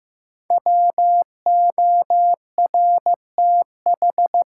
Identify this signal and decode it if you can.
Morse code, heard as a keyed tone, about 15 wpm, 700 Hz. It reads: WORTH
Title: Sentences from Top 800 Words - New Word Drill - 15wpm